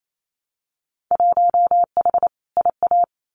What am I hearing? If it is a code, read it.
Morse code, heard as a keyed tone, about 28 wpm, 700 Hz. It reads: 1H IA
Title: Winter Field Day Exchanges 28wpm